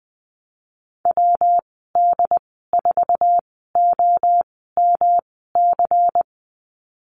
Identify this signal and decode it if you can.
Morse code, heard as a keyed tone, about 20 wpm, spaced as for 14 wpm, 700 Hz. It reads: WD4OMC
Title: 219 Lesson 35g - Number 0 - Call Signs 20wpm